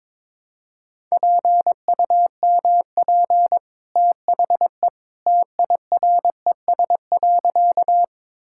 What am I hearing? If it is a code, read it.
Morse code, heard as a keyed tone, about 22 wpm, 700 Hz. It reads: PUMP THE TIRES.
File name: practice / 22wpm / Easy Sentences with Hints - 22wpm